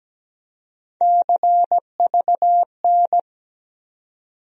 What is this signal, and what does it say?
Morse code, heard as a keyed tone, about 17 wpm, 700 Hz. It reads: CVN
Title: USA Call Sign Postfixes - 3 Letters 17wpm